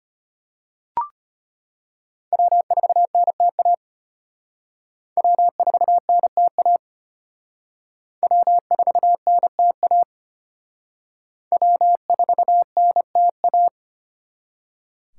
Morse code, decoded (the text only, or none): W4NTA W4NTA W4NTA W4NTA